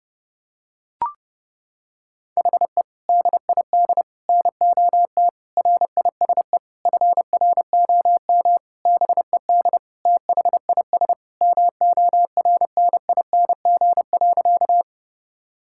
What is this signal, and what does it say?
Morse code, heard as a keyed tone, about 32 wpm, 700 Hz. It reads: HE DID NOT RISE FROM BED THIS MORNING.
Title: Sentences from Top 1000 Words - Review Part 6 - 30wpm